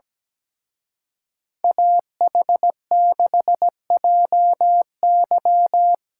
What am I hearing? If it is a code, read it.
Morse code, heard as a keyed tone, about 20 wpm, 700 Hz. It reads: AH6JY
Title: Speed Racing - USA Call Signs 17wpm